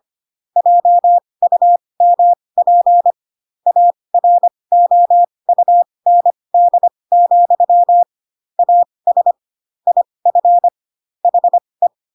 Morse code, heard as a keyed tone, about 25 wpm, spaced as for 20 wpm, 700 Hz. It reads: JUMP AROUND, AS IF HE
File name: Pinocchio / Pinocchio ch8 15 25 ik1hge